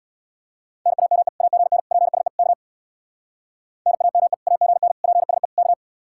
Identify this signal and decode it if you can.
Morse code, heard as a keyed tone, about 64 wpm, 700 Hz. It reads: NAME AGN PSE K NAME AGN PSE K